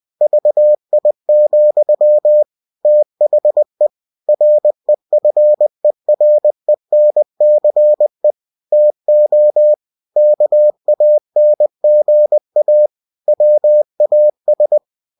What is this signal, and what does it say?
Morse code, heard as a keyed tone, about 20 wpm, 600 Hz. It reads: VI, THE REFERENCE TO KANGA WAS
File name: cw-books / winnie-the-pooh / winnie-20wpm-page-0111